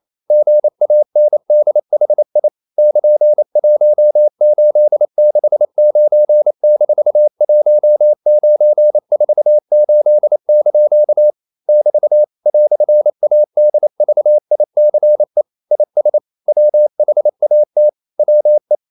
Morse code, heard as a keyed tone, about 28 wpm, 600 Hz. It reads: GANDHI <KN>1869-1948) <BT> "ADVICE IS WHAT WE
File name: CW / Practice / EnglishQuotations0228